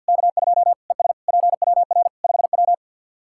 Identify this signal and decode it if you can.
Morse code, heard as a keyed tone, about 53 wpm, 700 Hz. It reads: K2 ES PWR 5W
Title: Practice QSO - Exchange 2 - Info - 50wpm